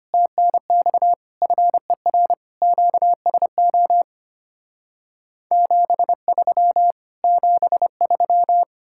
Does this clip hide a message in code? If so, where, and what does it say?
Yes: Morse code from 0.1 to 8.6 s, heard as a keyed tone, about 33 wpm, 700 Hz. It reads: TNX FER QSO 73 73